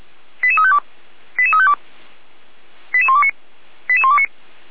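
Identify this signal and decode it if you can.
Five-tone selective call (ZVEI), heard as quick runs of five tones, tones of about 70 ms. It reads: 88352 88352 88129 88129
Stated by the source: ZVEI five-tone selective call received by SDR.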